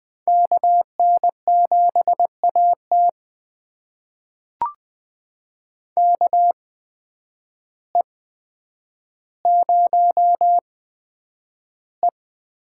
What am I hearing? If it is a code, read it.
Morse code, heard as a keyed tone, about 20 wpm, spaced as for 9 wpm, 700 Hz. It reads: KN7AT KE0E
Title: USA Call Signs - 8x Character Spacing - 20wpm